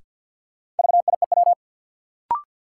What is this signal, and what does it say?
Morse code, heard as a keyed tone, about 53 wpm, 700 Hz. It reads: VIEW
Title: Sets of 5 Words - New Word Drill - 50wpm